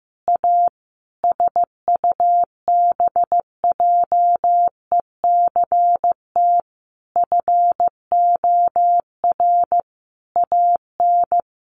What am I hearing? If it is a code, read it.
Morse code, heard as a keyed tone, about 15 wpm, 700 Hz. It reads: A SUBJECT FOR AN
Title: Sentences from Top 1500 Words 15wpm